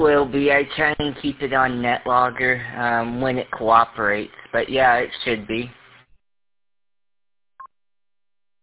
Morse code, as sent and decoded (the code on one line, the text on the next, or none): none